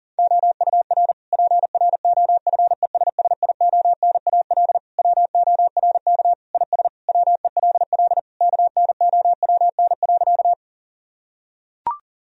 Morse code, heard as a keyed tone, about 37 wpm, 700 Hz. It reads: OUR PROFESSIONAL WORK IS WELL KNOWN.
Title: Sentences from Top 1000 Words - Review Part 4 - 40wpm